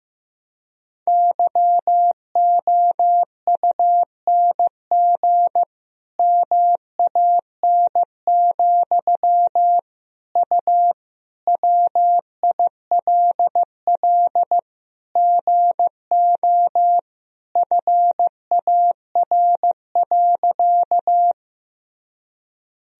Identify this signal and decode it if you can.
Morse code, heard as a keyed tone, about 15 wpm, 700 Hz. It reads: YOUNG MAN, U WILL GO FAR.
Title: Sentences from Top 400 Words and Abbreviations 15wpm